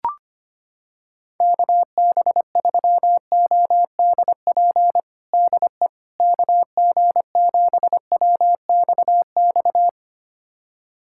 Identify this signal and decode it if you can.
Morse code, heard as a keyed tone, about 25 wpm, 700 Hz. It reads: KB3ODP DE KG7WXX
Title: USA Callsign DE Callsign 25wpm